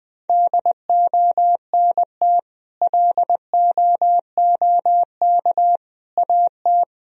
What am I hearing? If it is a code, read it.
Morse code, heard as a keyed tone, about 19 wpm, 700 Hz. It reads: DONT LOOK AT
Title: Sentences from Top 1000 Words - Review Part 1 - 20wpm